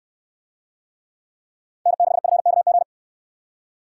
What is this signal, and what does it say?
Morse code, heard as a keyed tone, about 68 wpm, 690 Hz. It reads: N6FKG